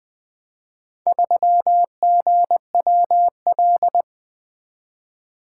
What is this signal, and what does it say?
Morse code, heard as a keyed tone, about 20 wpm, 700 Hz. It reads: 3GWL
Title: Four Letter-Number Combinations - 20wpm